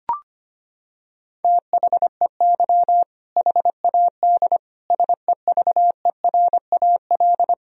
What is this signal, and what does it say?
Morse code, heard as a keyed tone, about 25 wpm, 700 Hz. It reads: THEY HAD SEVERAL
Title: Sentences from Top 1000 Words - Review Part 2 - 25wpm